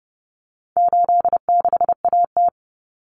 Morse code, heard as a keyed tone, about 29 wpm, 700 Hz. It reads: 86AT
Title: Common Cut Numbers - 4-Digit 30wpm